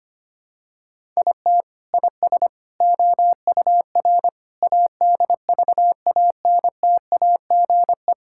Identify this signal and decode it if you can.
Morse code, heard as a keyed tone, about 25 wpm, 700 Hz. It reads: IT IS OUR ADVANTAGE
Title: Sentences from Top 1000 Words - Review Part 6 - 25wpm